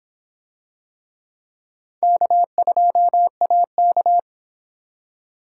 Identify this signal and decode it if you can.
Morse code, heard as a keyed tone, about 26 wpm, 700 Hz. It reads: K2AK